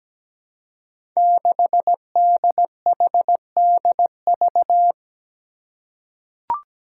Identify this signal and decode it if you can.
Morse code, heard as a keyed tone, about 17 wpm, 700 Hz. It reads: 6DHDV
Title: Five Challenging Letter-Number Combinations 17wpm